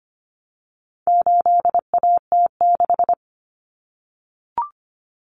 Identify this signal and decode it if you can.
Morse code, heard as a keyed tone, about 25 wpm, 700 Hz. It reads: 8AT6
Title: Common Cut Numbers - 4-Digit 25wpm